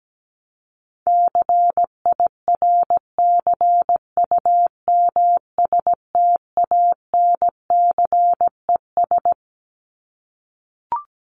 Morse code, heard as a keyed tone, about 17 wpm, 700 Hz. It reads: CIRCUMSTANCES
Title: ES:S Suffix - Top 100 Words with Base Words 17wpm